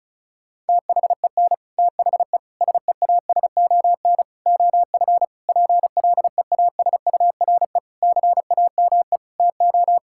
Morse code, heard as a keyed tone, about 36 wpm, 700 Hz. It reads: THEN THE SEASON OF PLEASURE CAME TO
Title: Sentences from Top 1600 Words 35wpm